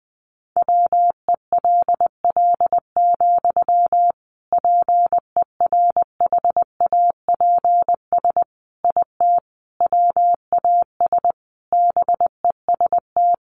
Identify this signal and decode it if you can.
Morse code, heard as a keyed tone, about 20 wpm, 700 Hz. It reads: WELL, PERHAPS IT WAS BEST
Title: Sentences from Top 500 Words 20wpm